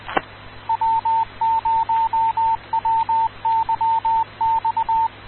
Morse code, heard as a keyed tone, about 20 wpm, 900 Hz, with radio noise. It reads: W0WYX
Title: Rocky Mountain Radio League Repeaters